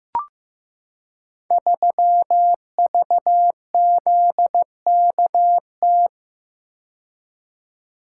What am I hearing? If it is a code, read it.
Morse code, heard as a keyed tone, about 15 wpm, 700 Hz. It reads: 3VZKT
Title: Code Groups - 5 Letter:Numbers - 15wpm